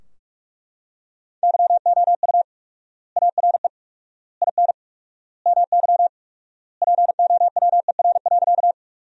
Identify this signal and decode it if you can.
Morse code, heard as a keyed tone, about 41 wpm, 700 Hz. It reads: YOU ARE IN MY POWER.
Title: 4x Word-Spacing - Sentences from Top 500 Words - 45wpm